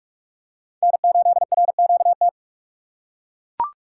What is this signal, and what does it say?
Morse code, heard as a keyed tone, about 42 wpm, 700 Hz. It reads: N8RQT
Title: USA Call Signs - 5 Characters - 4x Character Spacing - 45wpm